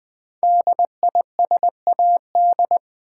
Morse code, heard as a keyed tone, about 20 wpm, 700 Hz. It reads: DISAD
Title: Build A Word - 5 Letters At A Time - 12-Letter Words - 20wpm